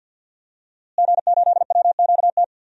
Morse code, heard as a keyed tone, about 53 wpm, 700 Hz. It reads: G8WQT
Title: DXCC-100 Call Signs - 5 Characters - Repeated 2x - 50wpm